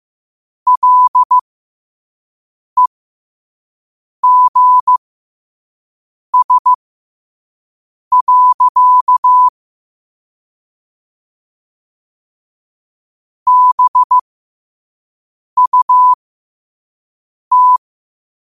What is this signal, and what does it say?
Morse code, heard as a keyed tone, about 15 wpm, spaced as for 5 wpm, 1000 Hz. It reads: LEGS. BUT
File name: qotdfiles / 5wpm